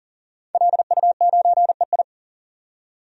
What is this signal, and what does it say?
Morse code, heard as a keyed tone, about 40 wpm, 700 Hz. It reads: LU9EI
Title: DXCC-100 Call Signs - 5 Characters - 40wpm